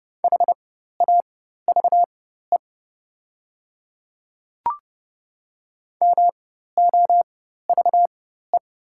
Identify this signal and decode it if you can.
Morse code, heard as a keyed tone, about 30 wpm, spaced as for 14 wpm, 700 Hz. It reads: HAVE MOVE
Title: 4x Character Spacing - Top 200 Words - 30wpm